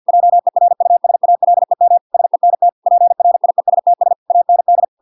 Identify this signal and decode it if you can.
Morse code, heard as a keyed tone, about 50 wpm, 700 Hz. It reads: JERUSALEM SENT PRIESTS AND